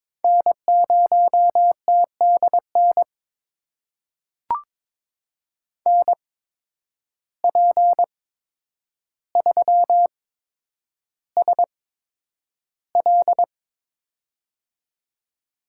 Morse code, heard as a keyed tone, about 22 wpm, spaced as for 9 wpm, 700 Hz. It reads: N0TDN NP3SL
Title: USA Call Signs - 5 Characters - 8x Character Spacing - 22wpm